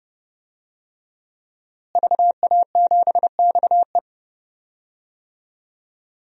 Morse code, heard as a keyed tone, about 29 wpm, 700 Hz. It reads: VA7XE
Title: DXCC-100 Call Signs - 5 Characters - Repeated 3x - 30wpm